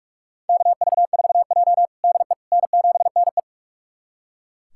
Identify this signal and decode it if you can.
Morse code, heard as a keyed tone, about 43 wpm, 700 Hz. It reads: KV4J DE N7NE
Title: USA Callsign DE Callsign - 4 Characters - 45wpm